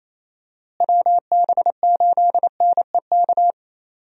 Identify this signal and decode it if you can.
Morse code, heard as a keyed tone, about 27 wpm, 700 Hz. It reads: WB8NEK
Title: USA Call Signs - 6 Characters - Repeated 2x - 28wpm